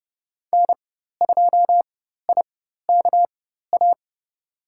Morse code, heard as a keyed tone, about 31 wpm, spaced as for 14 wpm, 700 Hz. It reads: N2IKA
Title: USA Call Signs - 5 Characters - 4x Character Spacing - 30wpm